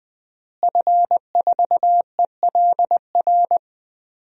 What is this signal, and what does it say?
Morse code, heard as a keyed tone, about 21 wpm, 700 Hz. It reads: F4ELR
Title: DXCC-100 Call Signs - 5 Characters - Repeated 3x - 20wpm